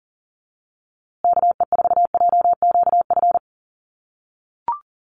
Morse code, heard as a keyed tone, about 40 wpm, 700 Hz. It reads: KE4JQF
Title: USA Call Signs Repeated 3x 40wpm